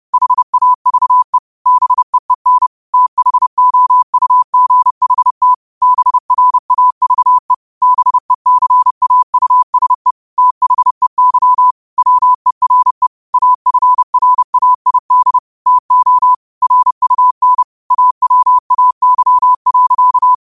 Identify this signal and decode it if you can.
Morse code, heard as a keyed tone, about 30 wpm, 1000 Hz. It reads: HAVE BEEN THOUGHT BRAVE BECAUSE THEY WERE AFRAID TO RUN AWAY.
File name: 30wpm